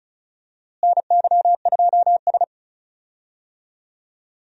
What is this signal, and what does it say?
Morse code, heard as a keyed tone, about 36 wpm, 700 Hz. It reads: NY2S
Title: USA Call Signs - 4 Characters - Repeated 2x - 35wpm